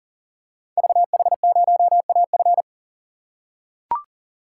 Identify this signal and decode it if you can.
Morse code, heard as a keyed tone, about 41 wpm, 700 Hz. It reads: VH0AF
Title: Code Groups - 5 Letter:Numbers - 40wpm